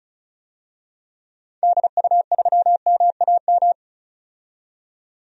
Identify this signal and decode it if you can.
Morse code, heard as a keyed tone, about 34 wpm, 700 Hz. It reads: DU3MAM
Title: DXCC-100 Call Signs - 6 Characters - 35wpm